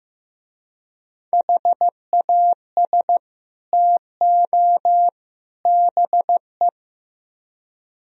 Morse code, heard as a keyed tone, about 18 wpm, 700 Hz. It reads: HAS TO BE
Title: Speed-Racing - Sets of 3 Words 15wpm